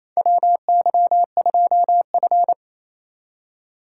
Morse code, heard as a keyed tone, about 28 wpm, 700 Hz. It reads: WY2F
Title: USA Call Signs - 4x Character Spacing - 28wpm